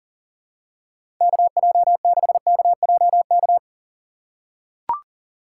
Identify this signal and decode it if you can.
Morse code, heard as a keyed tone, about 37 wpm, 700 Hz. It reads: KJ6KJK
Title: USA Call Signs - 6 Characters - 3x Character Spacing - 40wpm